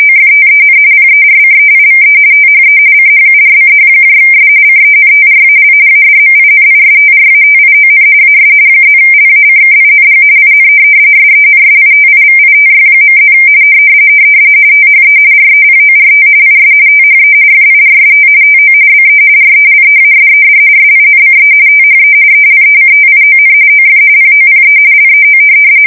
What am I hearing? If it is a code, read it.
RTTY, about 45.45 baud. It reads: OFTEN PERCEIVE THE CHURCHS CENTRAL ADMINISTRATION AS +LACKING IN SENSITIVITY.+ THE REPORT IS THE FIRST TO BE RELEASED SINCE THE ELECTION OF POPE LEO
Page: 170 Shift 2125/2295